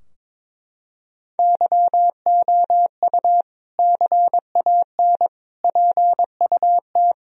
Morse code, heard as a keyed tone, about 22 wpm, 700 Hz. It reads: YOU CAN PUT
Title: Sentences from Top 1000 Words - Review Part 2 - 22wpm